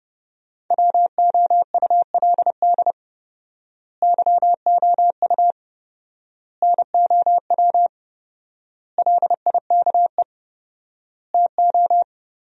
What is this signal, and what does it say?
Morse code, heard as a keyed tone, about 30 wpm, 700 Hz. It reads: WOULD YOU NOW LIKE TO
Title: 4x Word-Spacing - Sentences from Top 200 Words - 30wpm